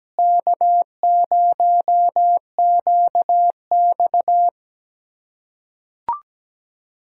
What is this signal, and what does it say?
Morse code, heard as a keyed tone, about 17 wpm, 700 Hz. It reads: K0QX
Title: USA Callsign DE Callsign - 4 Characters - 17wpm